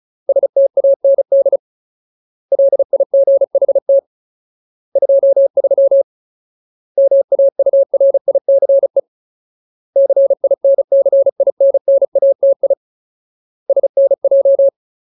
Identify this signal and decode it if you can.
Morse code, heard as a keyed tone, about 34 wpm, 550 Hz. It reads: STAND LIGHT 23 MAURICE CINCINNATI SNJ
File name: Ultimate Word List / 35WPM Ultimate